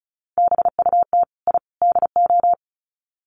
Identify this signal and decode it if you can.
Morse code, heard as a keyed tone, about 34 wpm, 700 Hz. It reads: BUT I DO
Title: Sets of 3 Words 35wpm